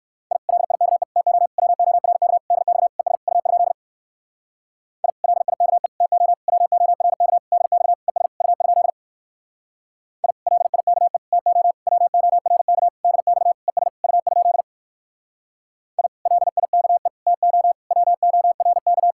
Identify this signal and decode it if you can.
Morse code, heard as a keyed tone, about 67 wpm, 700 Hz. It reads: I LIKE TO WORK DX ES U? I LIKE TO WORK DX ES U? I LIKE TO WORK DX ES U? I LIKE TO WORK